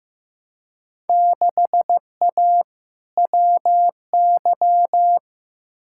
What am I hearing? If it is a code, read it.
Morse code, heard as a keyed tone, about 15 wpm, 700 Hz. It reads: 6A WY
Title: Field Day Exchanges Encore - 15wpm